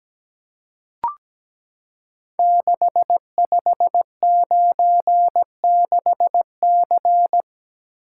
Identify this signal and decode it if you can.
Morse code, heard as a keyed tone, about 17 wpm, 700 Hz. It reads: 6596C
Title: Code Groups - 5 Letter:Numbers - 17wpm